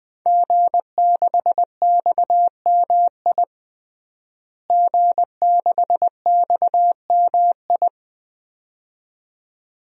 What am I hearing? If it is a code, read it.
Morse code, heard as a keyed tone, about 20 wpm, 700 Hz. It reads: G6XMI G6XMI
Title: DXCC-100 Call Signs - 5 Characters - Repeated 2x - 20wpm